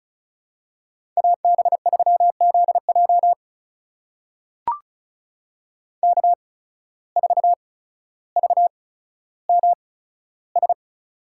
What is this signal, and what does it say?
Morse code, heard as a keyed tone, about 36 wpm, spaced as for 15 wpm, 700 Hz. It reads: AB3ZJ K4VMS
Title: USA Call Signs - 5 Characters - 8x Character Spacing - 35wpm